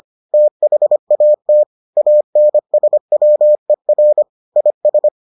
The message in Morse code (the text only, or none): THAT ANSWER IS